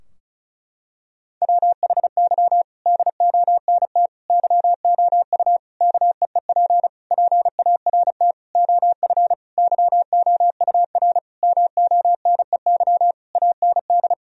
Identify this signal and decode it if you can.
Morse code, heard as a keyed tone, about 35 wpm, 700 Hz. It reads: WHY DONT YOU KEEP PART OF YOUR MONEY AND
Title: Sentences from Top 1000 Words - Review Part 4 - 35wpm